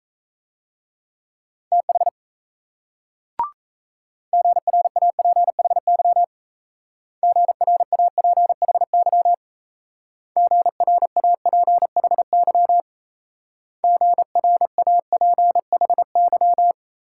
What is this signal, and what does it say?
Morse code, heard as a keyed tone, about 37 wpm, 700 Hz. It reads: TH GRAPHY GRAPHY GRAPHY GRAPHY